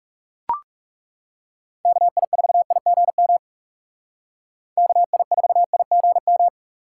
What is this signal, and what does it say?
Morse code, heard as a keyed tone, about 38 wpm, 700 Hz. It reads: KI4IGM KI4IGM